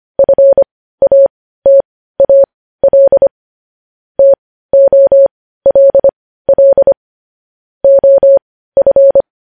Morse code, heard as a keyed tone, about 25 wpm, spaced as for 15 wpm, 550 Hz. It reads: FATAL TOLL OF
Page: News Headlines in Morse Code at 15 WPM